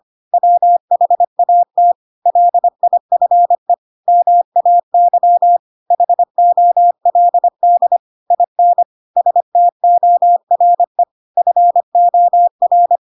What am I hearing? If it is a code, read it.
Morse code, heard as a keyed tone, about 25 wpm, 700 Hz. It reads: WHAT LIFE MAY HOLD IN STORE FOR
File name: Pinocchio / Pinocchio ch7 25 25 ik1hge